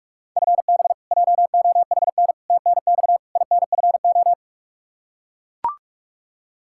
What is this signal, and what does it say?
Morse code, heard as a keyed tone, about 42 wpm, 700 Hz. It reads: FB JOHN TNX INFO
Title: Practice QSO - Exchange 3 - More Info - 45wpm